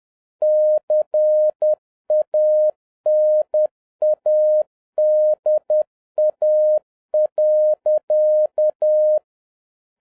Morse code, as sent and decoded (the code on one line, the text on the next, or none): -.-. .- -. .- -.. .- .-.-.-
CANADA.